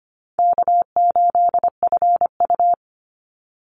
Morse code, heard as a keyed tone, about 25 wpm, 700 Hz. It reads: K8FU
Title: Easy Call Signs - 25wpm